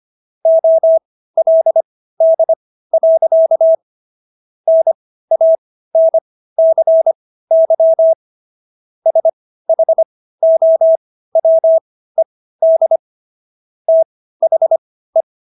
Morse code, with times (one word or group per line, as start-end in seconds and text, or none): 0.5-3.8 OLD.
4.7-8.1 NANCY
9.1-13.0 SHOWED
13.9-15.2 THE